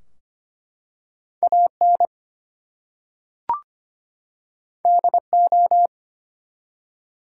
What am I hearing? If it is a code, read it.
Morse code, heard as a keyed tone, about 25 wpm, 700 Hz. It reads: AN DO
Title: Two Letter Words - 25wpm